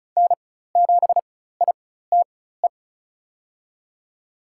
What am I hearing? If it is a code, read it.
Morse code, heard as a keyed tone, about 34 wpm, spaced as for 16 wpm, 700 Hz. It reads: N7ITE
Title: USA Call Signs - 5 Characters - 4x Character Spacing - 35wpm